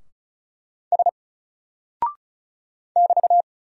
Morse code, heard as a keyed tone, about 35 wpm, 700 Hz. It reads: S <BT>
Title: Single Character (Letter:Number:Punctuation:Prosign) 35wpm